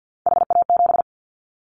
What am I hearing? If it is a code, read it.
Morse code, heard as a keyed tone, about 50 wpm, 700 Hz. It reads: HA7
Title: Callsign Prefixes for Top 25 DXCC - Repeated 3x - 50wpm